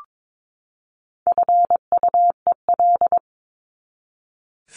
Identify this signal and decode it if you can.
Morse code, heard as a keyed tone, about 22 wpm, 700 Hz. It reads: FUEL
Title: Top 1751-2000 Words 22wpm